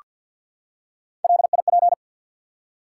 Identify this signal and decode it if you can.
Morse code, heard as a keyed tone, about 48 wpm, 700 Hz. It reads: LIP
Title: Three Letter Words - 50wpm